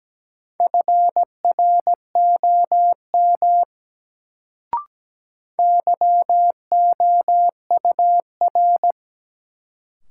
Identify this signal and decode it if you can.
Morse code, heard as a keyed tone, about 17 wpm, 700 Hz. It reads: FROM YOUR
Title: Sentences from Top 100 Words - New Word Drill - 17wpm